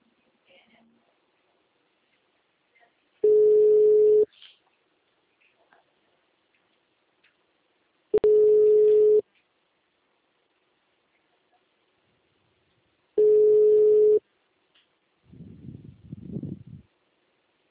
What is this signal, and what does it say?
Ringback tone (Europe, 425 Hz), with speech.